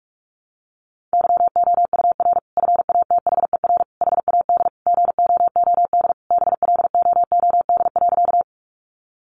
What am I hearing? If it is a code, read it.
Morse code, heard as a keyed tone, about 46 wpm, 700 Hz. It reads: YOUR FATHER HAD GOOD BLOOD.
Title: Sentences from Top 1100 Words 45wpm